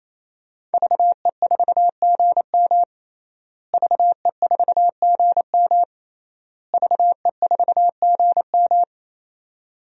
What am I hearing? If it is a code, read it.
Morse code, heard as a keyed tone, about 28 wpm, 700 Hz. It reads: VE4GM VE4GM VE4GM
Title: DXCC-100 Call Signs - 5 Characters - Repeated 3x - 28wpm